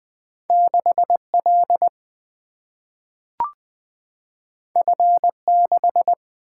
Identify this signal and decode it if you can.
Morse code, heard as a keyed tone, about 20 wpm, 700 Hz. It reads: 6 L F 6
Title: Single-Letter Single-Number 20wpm